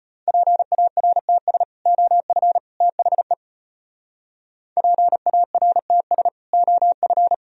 Morse code, heard as a keyed tone, about 34 wpm, 700 Hz. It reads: PARTS OF THE PARTS OF